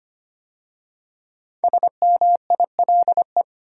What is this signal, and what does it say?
Morse code, heard as a keyed tone, about 25 wpm, 700 Hz. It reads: SMILE
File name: 5 Letter Words - First 3 Letters Given - 25wpm